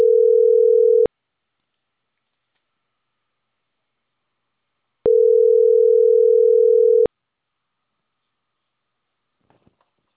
Ringback tone (North America), 440 + 480 Hz.